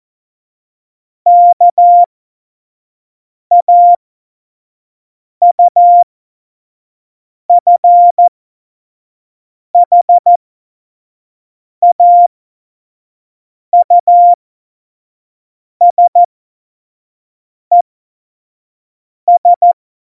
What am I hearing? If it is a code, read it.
Morse code, heard as a keyed tone, about 14 wpm, spaced as for 5 wpm, 700 Hz. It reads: KAUFHAUSES
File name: Morsekurs / Witze / Witz 08. Mai 2018, 25 F70